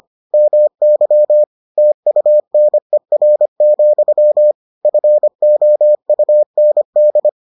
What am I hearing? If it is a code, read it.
Morse code, heard as a keyed tone, about 25 wpm, 600 Hz. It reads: MY TUNER, FOUND